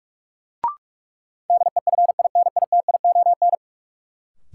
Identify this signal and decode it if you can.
Morse code, heard as a keyed tone, about 46 wpm, 700 Hz. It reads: DEFINITION
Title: Sentences from Top 1000 Words - Review Part 4 - New Word Drill - 45wpm